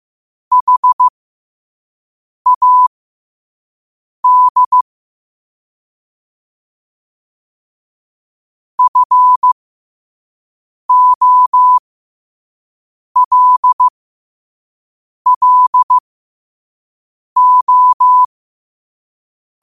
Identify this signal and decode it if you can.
Morse code, heard as a keyed tone, about 15 wpm, spaced as for 5 wpm, 1000 Hz. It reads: HAD FOLLO
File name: qotdfiles / 5wpm